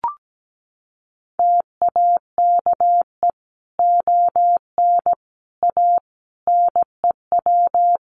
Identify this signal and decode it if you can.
Morse code, heard as a keyed tone, about 17 wpm, 700 Hz. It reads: TAKE ON A NEW
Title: Sentences from Top 1400 Words 17wpm